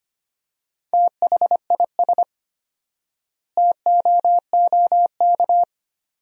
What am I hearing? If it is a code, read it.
Morse code, heard as a keyed tone, about 25 wpm, 700 Hz. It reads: THIS TOOK
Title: 4x Word-Spacing - Sentences from Top 300 Words - 25wpm